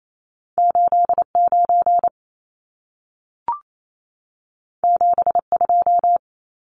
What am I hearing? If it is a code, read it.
Morse code, heard as a keyed tone, about 28 wpm, 700 Hz. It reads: 89 72
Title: Numbers - 2-Digit 28wpm